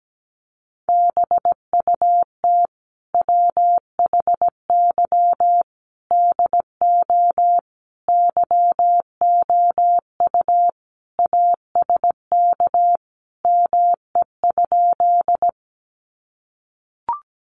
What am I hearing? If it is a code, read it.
Morse code, heard as a keyed tone, about 17 wpm, 700 Hz. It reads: BUT WHY DO YOU ASK ME?
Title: Sentences from Top 200 Words 17wpm